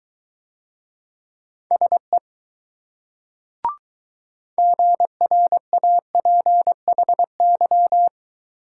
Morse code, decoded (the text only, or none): SE GRAPHY